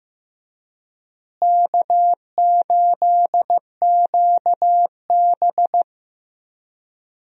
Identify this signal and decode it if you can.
Morse code, heard as a keyed tone, about 15 wpm, 700 Hz. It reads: K8QB
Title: USA Call Signs - 4 Characters - Repeated 3x - 15wpm